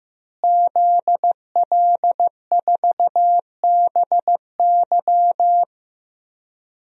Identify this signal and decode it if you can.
Morse code, heard as a keyed tone, about 15 wpm, 700 Hz. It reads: ZL4BY
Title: DXCC-100 Call Signs - 5 Characters - Repeated 2x - 15wpm